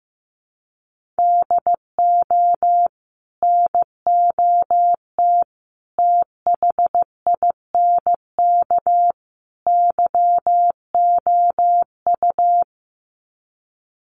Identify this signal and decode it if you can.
Morse code, heard as a keyed tone, about 15 wpm, 700 Hz. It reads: DO NOT THINK YOU
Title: Sets of 4 Words 15wpm